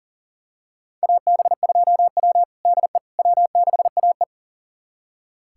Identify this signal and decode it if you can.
Morse code, heard as a keyed tone, about 40 wpm, 700 Hz. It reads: AB2W DE W6AE
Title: USA Callsign DE Callsign - 4 Characters - 40wpm